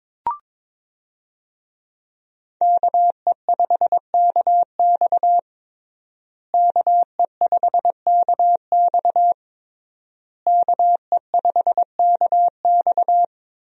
Morse code, heard as a keyed tone, about 22 wpm, 700 Hz. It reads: KE5KX KE5KX KE5KX
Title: USA Call Signs - 5 Characters - Repeated 3x - 22wpm